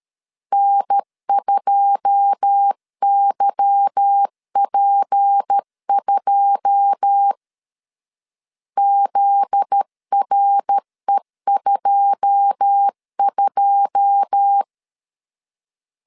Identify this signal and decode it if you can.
Morse code, heard as a keyed tone, about 13 wpm, 800 Hz. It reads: N2YP2 ZRE22
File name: LEZIONI / 9 2 12wpm 2301